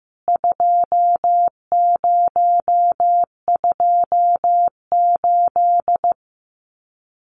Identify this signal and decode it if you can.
Morse code, heard as a keyed tone, about 15 wpm, 700 Hz. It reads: 2028
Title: CWT Exchange 15wpm